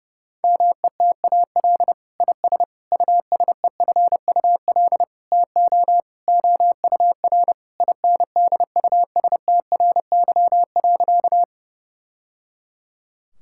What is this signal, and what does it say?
Morse code, heard as a keyed tone, about 32 wpm, 700 Hz. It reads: METAL IS USEFUL TO OUR INDUSTRY.
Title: Sentences from Top 1000 Words - Review Part 6 - 30wpm